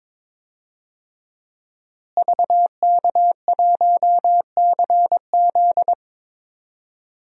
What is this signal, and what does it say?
Morse code, heard as a keyed tone, about 22 wpm, 700 Hz. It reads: VK1CZ
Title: DXCC-100 Call Signs - 5 Characters - Repeated 3x - 22wpm